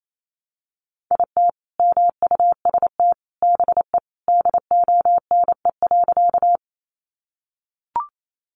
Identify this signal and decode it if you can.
Morse code, heard as a keyed tone, about 28 wpm, 700 Hz. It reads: IT MUST BE DONE.
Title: Sentences from Top 500 Words 28wpm